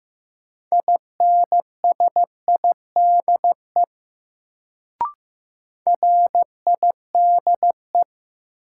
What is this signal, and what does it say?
Morse code, heard as a keyed tone, about 15 wpm, 700 Hz. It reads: INSIDE RIDE
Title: Sentences from Top 1200 Words - New Word Drill Alt - 15wpm